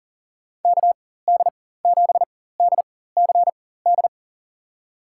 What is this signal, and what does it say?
Morse code, heard as a keyed tone, about 40 wpm, spaced as for 19 wpm, 690 Hz. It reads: KD7DCD
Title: USA Call Signs - 4x Character Spacing - 40wpm